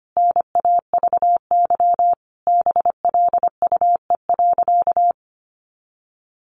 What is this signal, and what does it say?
Morse code, heard as a keyed tone, about 25 wpm, 700 Hz. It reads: NAVY BLUE.
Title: Additional Easy Phrases 25wpm